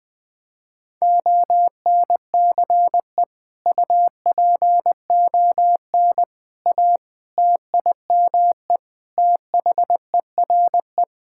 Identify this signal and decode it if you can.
Morse code, heard as a keyed tone, about 21 wpm, 700 Hz. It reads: ONCE UPON A TIME THERE
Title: Sentences with 7-Word Phrases - 20wpm